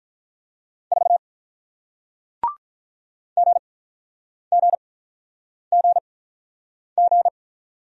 Morse code, heard as a keyed tone, about 48 wpm, spaced as for 9 wpm, 700 Hz. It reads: 4 GGGG